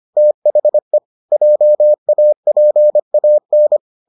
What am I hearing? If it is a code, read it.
Morse code, heard as a keyed tone, about 25 wpm, 600 Hz. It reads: THE JAPAN